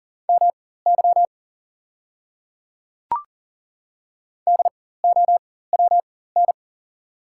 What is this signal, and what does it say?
Morse code, heard as a keyed tone, about 40 wpm, spaced as for 19 wpm, 690 Hz. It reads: MY DOWN
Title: 4x Character Spacing - Top 200 Words - 40wpm